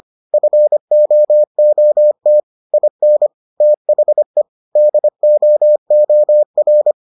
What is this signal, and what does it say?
Morse code, heard as a keyed tone, about 25 wpm, 600 Hz. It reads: FOOT IN THE DOOR